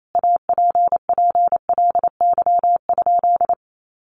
Morse code, heard as a keyed tone, about 28 wpm, 700 Hz. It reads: APPLY?
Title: Sentences from Top 1200 Words 28wpm